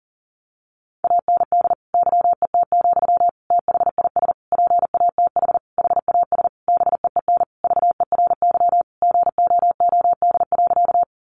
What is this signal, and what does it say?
Morse code, heard as a keyed tone, about 40 wpm, 700 Hz. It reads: AND YET, THIS PATH HAS BEEN VERY GOOD.
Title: Sentences from Top 1400 Words 40wpm